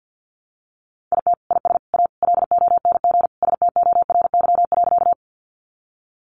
Morse code, heard as a keyed tone, about 50 wpm, 700 Hz. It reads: IT IS A LONG STORY.
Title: Sentences from Top 300 Words 50wpm